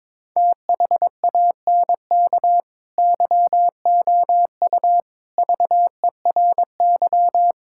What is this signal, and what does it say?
Morse code, heard as a keyed tone, about 22 wpm, 700 Hz. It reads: THANK YOU VERY
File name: practice / 22wpm / Sentences with 4-Word Phrases - 22wpm